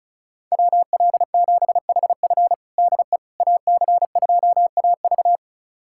Morse code, heard as a keyed tone, about 34 wpm, 700 Hz. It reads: WL7HF DE AC2AV
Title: USA Callsign DE Callsign - 5 Characters - 35wpm